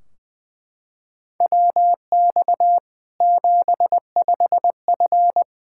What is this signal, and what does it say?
Morse code, heard as a keyed tone, about 20 wpm, 700 Hz. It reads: WX 75F
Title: K1USN SST QSO Practice 20wpm